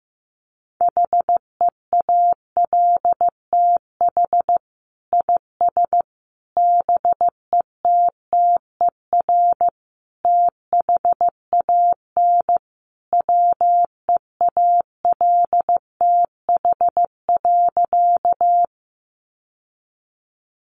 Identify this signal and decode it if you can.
Morse code, heard as a keyed tone, about 15 wpm, 700 Hz. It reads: HEALTH IS BETTER THAN WEALTH.
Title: Sentences from Top 1200 Words 15wpm